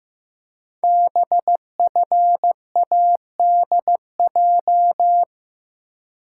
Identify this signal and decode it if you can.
Morse code, heard as a keyed tone, about 15 wpm, 700 Hz. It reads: BFADJ
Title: Code Groups - 5 Letters - 15wpm